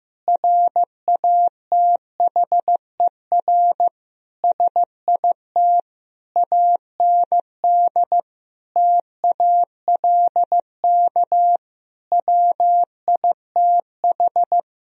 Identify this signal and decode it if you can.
Morse code, heard as a keyed tone, about 15 wpm, 700 Hz. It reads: RATHER SIT AND TALK WITH
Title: Sentences from Top 1000 Words - Review Part 4 - 15wpm